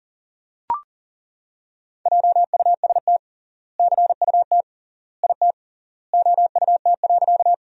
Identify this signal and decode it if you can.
Morse code, heard as a keyed tone, about 40 wpm, 700 Hz. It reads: JUST CUT IT OUT.
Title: 3x Word-Spacing - Sentences from Top 300 Words - 40wpm